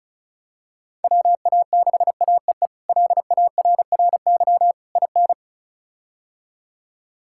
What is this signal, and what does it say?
Morse code, heard as a keyed tone, about 34 wpm, 700 Hz. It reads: WA6AEE LARRY IN
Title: K1USN SST QSO Practice - 35wpm